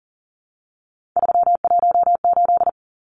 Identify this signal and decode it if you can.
Morse code, heard as a keyed tone, about 40 wpm, 700 Hz. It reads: 318
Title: Numbers - 3-Digit 40wpm